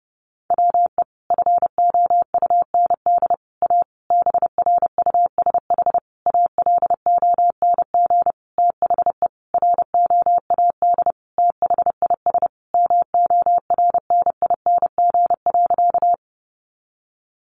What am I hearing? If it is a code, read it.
Morse code, heard as a keyed tone, about 30 wpm, 700 Hz. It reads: WE FOUND A BRUSH ALONG THE ROAD THIS MORNING.
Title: Sentences from Top 1100 Words 30wpm